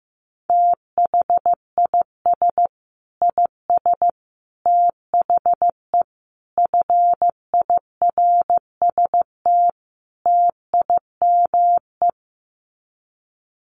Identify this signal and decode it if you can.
Morse code, heard as a keyed tone, about 14 wpm, 700 Hz. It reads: THIS IS THE FIRST TIME
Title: Sets of 5 Words 15wpm